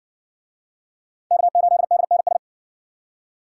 Morse code, heard as a keyed tone, about 56 wpm, 700 Hz. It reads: K8DNS